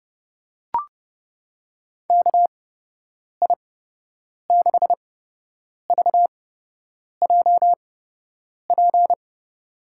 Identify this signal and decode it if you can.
Morse code, heard as a keyed tone, about 30 wpm, spaced as for 8 wpm, 700 Hz. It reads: KI6VJP
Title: USA Call Signs - 6 Characters - 8x Character Spacing - 30wpm